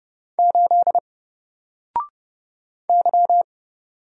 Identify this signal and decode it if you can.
Morse code, heard as a keyed tone, about 30 wpm, 700 Hz. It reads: 8 Y
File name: Single-Letter-Number-30wpm